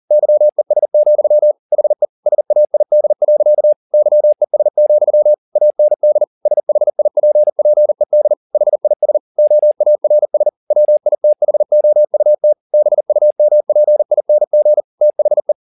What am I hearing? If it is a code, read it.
Morse code, heard as a keyed tone, about 40 wpm, 600 Hz. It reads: YES, HE SAID. YES, AND SHIPPED HIS OARS WITHOUT BUMPING THE